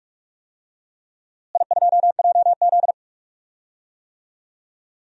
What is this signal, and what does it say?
Morse code, heard as a keyed tone, about 41 wpm, 700 Hz. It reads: I2JZ
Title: DXCC-100 Call Signs - 4 Characters - Repeated 3x - 45wpm